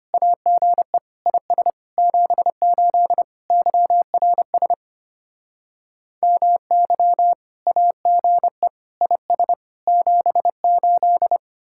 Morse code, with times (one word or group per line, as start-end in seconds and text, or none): 0.1-1.0 AGE
1.3-1.7 IS
2.0-3.2 78
3.5-4.7 YRS
6.2-7.3 MY
7.7-8.7 AGE
9.0-9.5 IS
9.9-11.4 78